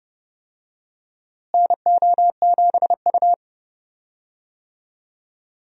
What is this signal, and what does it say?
Morse code, heard as a keyed tone, about 31 wpm, 700 Hz. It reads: NO7U
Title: USA Call Signs - 4 Characters - Repeated 2x - 30wpm